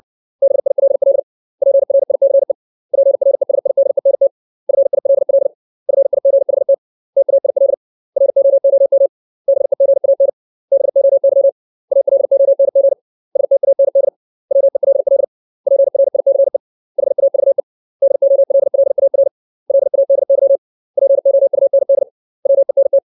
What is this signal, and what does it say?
Morse code, heard as a keyed tone, about 60 wpm, 550 Hz. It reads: BILL PRICE PRESIDENT FILL FIGHT TAIL ROOM BEGAN BOY ALONG STAND WELL PRICE HAVE DOLLAR LADY POUND WENT
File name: 632 words / 632 Words 60WPM